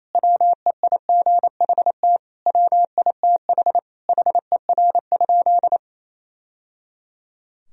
Morse code, heard as a keyed tone, about 28 wpm, 700 Hz. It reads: WEIGHT WITH HER?
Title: Sentences from Top 1000 Words - Review Part 3 - 28wpm